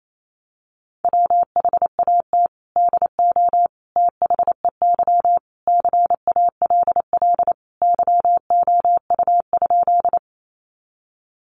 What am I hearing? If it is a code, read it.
Morse code, heard as a keyed tone, about 28 wpm, 700 Hz. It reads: WHAT DO THEY CALL YOU?
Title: Sentences from Top 100 Words 28wpm